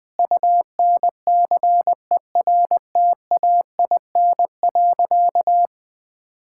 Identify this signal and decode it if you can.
Morse code, heard as a keyed tone, about 21 wpm, 700 Hz. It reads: UNCERTAIN.
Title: Sentences with 3-Word Phrases - 20wpm